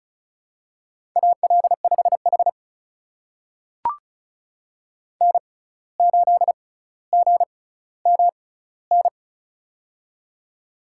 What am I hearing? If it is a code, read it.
Morse code, heard as a keyed tone, about 36 wpm, spaced as for 17 wpm, 700 Hz. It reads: AL5H N8GMN
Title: USA Call Signs - 6x Character Spacing - 35wpm